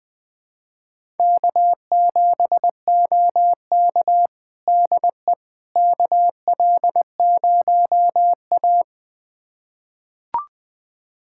Morse code, heard as a keyed tone, about 20 wpm, 700 Hz. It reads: K7OK DE KL0A
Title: USA Callsign DE Callsign - 4 Characters - 20wpm